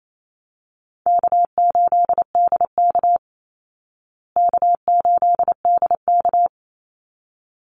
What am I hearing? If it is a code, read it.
Morse code, heard as a keyed tone, about 28 wpm, 700 Hz. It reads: K8DK K8DK
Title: Easy Call Signs Repeated 2x - 28wpm